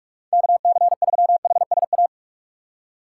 Morse code, heard as a keyed tone, about 44 wpm, 700 Hz. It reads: KC3HSA
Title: USA Call Signs - 6 Characters - 4x Character Spacing - 45wpm